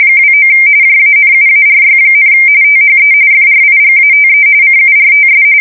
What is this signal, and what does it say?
RTTY, about 45.45 baud. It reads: 29 JANUARY, THE FRIGATE CURTS
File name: fsk170 / USS-Missouri